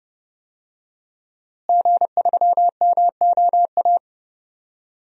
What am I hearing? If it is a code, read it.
Morse code, heard as a keyed tone, about 30 wpm, 700 Hz. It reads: G3MOA
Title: DXCC-100 Call Signs - 5 Characters - Repeated 3x - 30wpm